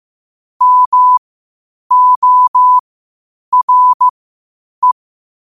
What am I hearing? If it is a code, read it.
Morse code, heard as a keyed tone, about 15 wpm, spaced as for 8 wpm, 1000 Hz. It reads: MORE
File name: qotdfiles / 8wpm